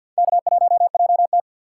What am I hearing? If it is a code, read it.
Morse code, heard as a keyed tone, about 52 wpm, 700 Hz. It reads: K1JT
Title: USA Call Signs - 4 Characters - 2x Character Spacing - 50wpm